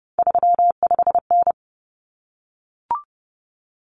Morse code, heard as a keyed tone, about 30 wpm, 700 Hz. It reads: 35N
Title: Common Cut Numbers - 3-Digit 30wpm